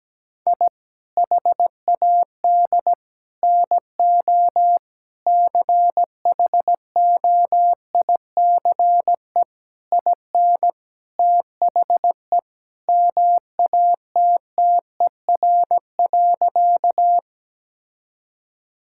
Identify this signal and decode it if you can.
Morse code, heard as a keyed tone, about 17 wpm, 700 Hz. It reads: I HAD NO CHOICE IN THE MATTER.
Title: Sentences from Top 1000 Words - Review Part 6 - 17wpm